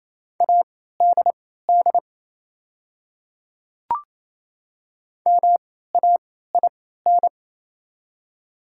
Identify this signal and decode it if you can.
Morse code, heard as a keyed tone, about 28 wpm, spaced as for 16 wpm, 700 Hz. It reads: ADD MAIN
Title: 3x Character Spacing - Top 300 Words - 28wpm